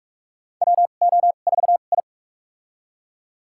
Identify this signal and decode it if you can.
Morse code, heard as a keyed tone, about 46 wpm, spaced as for 32 wpm, 700 Hz. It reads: WO4I
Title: USA Call Signs - 2x Character Spacing - 45wpm